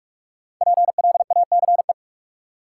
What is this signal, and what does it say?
Morse code, heard as a keyed tone, about 42 wpm, 700 Hz. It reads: PLACE
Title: Sets of 4 Words - New Word Drill - 45wpm